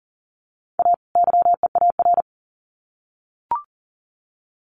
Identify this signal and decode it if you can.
Morse code, heard as a keyed tone, about 40 wpm, 700 Hz. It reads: A YEAR
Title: Sets of 2 Words 40wpm